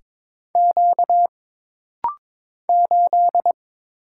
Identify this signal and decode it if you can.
Morse code, heard as a keyed tone, about 22 wpm, 700 Hz. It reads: Q 8
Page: This rapid-fire timing is geared to push your ICR (Instant Character Recognition) skills.